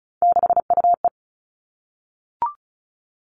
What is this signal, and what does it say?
Morse code, heard as a keyed tone, about 36 wpm, 700 Hz. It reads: 6UE
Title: Three Challenging Letter-Number Combinations 35wpm